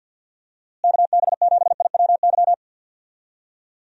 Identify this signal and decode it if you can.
Morse code, heard as a keyed tone, about 48 wpm, 700 Hz. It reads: KB7IWY
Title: USA Call Signs - 6 Characters - Repeated 3x - 50wpm